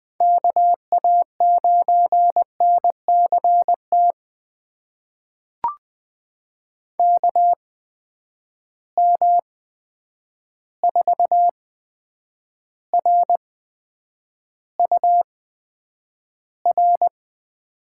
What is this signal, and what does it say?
Morse code, heard as a keyed tone, about 20 wpm, spaced as for 9 wpm, 700 Hz. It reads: KA9NCT KM4RUR
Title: USA Call Signs - 6 Characters - 8x Character Spacing - 20wpm